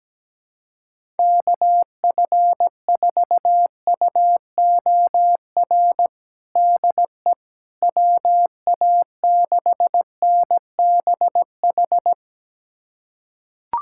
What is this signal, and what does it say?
Morse code, heard as a keyed tone, about 17 wpm, 700 Hz. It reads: KF4UOR DE WA6NBH
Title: USA Callsign DE Callsign - 6 Characters - 17wpm